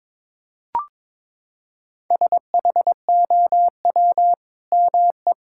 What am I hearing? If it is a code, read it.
Morse code, heard as a keyed tone, about 22 wpm, 700 Hz. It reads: SHOW ME
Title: Sentences from Top 1600 Words 22wpm